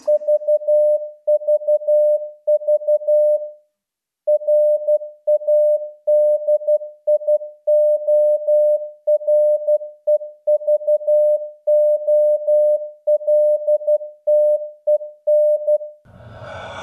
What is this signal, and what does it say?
Morse code, heard as a keyed tone, about 12 wpm, 620 Hz. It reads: VVV RADIOREVOLTEN